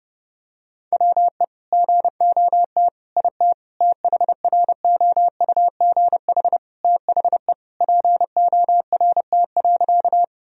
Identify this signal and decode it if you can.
Morse code, heard as a keyed tone, about 30 wpm, 700 Hz. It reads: WE GOT IT THROUGH THE PORT.
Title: Sentences from Top 1000 Words - Review Part 1 - 30wpm